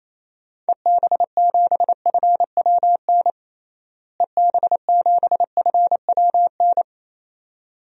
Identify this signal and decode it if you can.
Morse code, heard as a keyed tone, about 28 wpm, 700 Hz. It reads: EB7FWN EB7FWN
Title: DXCC-100 Call Signs - 6 Characters - Repeated 2x - 28wpm